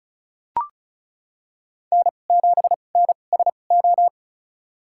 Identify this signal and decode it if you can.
Morse code, heard as a keyed tone, about 36 wpm, spaced as for 26 wpm, 700 Hz. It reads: N7NSO
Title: USA Call Signs - 5 Characters - 2x Character Spacing - 35wpm